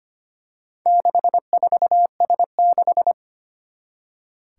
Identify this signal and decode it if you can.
Morse code, heard as a keyed tone, about 25 wpm, 700 Hz. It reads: 64S6
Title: Four Challenging Letter-Number Combinations 25wpm